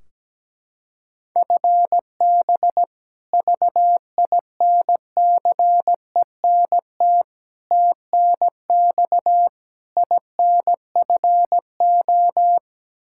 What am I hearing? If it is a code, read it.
Morse code, heard as a keyed tone, about 17 wpm, 700 Hz. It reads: FB VINCENT TNX INFO
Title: Practice QSO - Exchange 3 - More Info - 17wpm